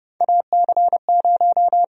Morse code, heard as a keyed tone, about 30 wpm, 700 Hz. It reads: AC0
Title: USA Call Sign Prefixes - Repeated 2x - 30wpm